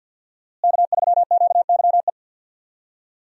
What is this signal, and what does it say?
Morse code, heard as a keyed tone, about 47 wpm, 700 Hz. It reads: K3QYE
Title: USA Call Signs - 5 Characters - Repeated 3x - 50wpm